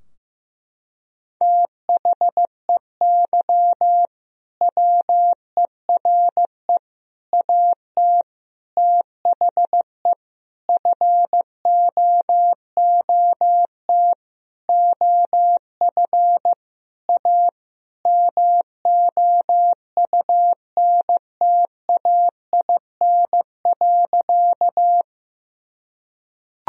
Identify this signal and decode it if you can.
Morse code, heard as a keyed tone, about 15 wpm, 700 Hz. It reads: THEY WERE AT THE FOOT OF A MOUNTAIN.
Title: Sentences from Top 1000 Words - Review Part 3 - 15wpm